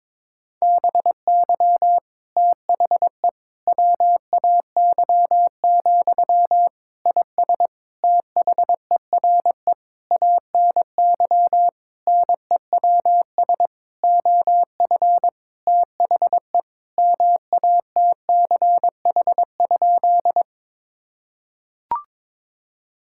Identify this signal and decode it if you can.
Morse code, heard as a keyed tone, about 22 wpm, 700 Hz. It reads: BY THE WAY, IS THERE ANY NEWS OF THE MATCH?
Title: Sentences from Top 1600 Words 22wpm